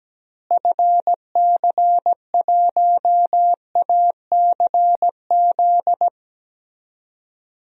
Morse code, heard as a keyed tone, about 17 wpm, 700 Hz. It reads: FC1ACZ
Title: DXCC-100 Call Signs - 6 Characters - Repeated 2x - 17wpm